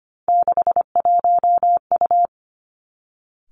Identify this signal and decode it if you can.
Morse code, heard as a keyed tone, about 25 wpm, 700 Hz. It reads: 61U
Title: Three Challenging Letter-Number Combinations 25wpm